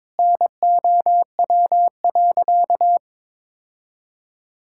Morse code, heard as a keyed tone, about 22 wpm, 700 Hz. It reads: NOW.
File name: practice / 22wpm / Sentences with 8-Word Phrases - 22wpm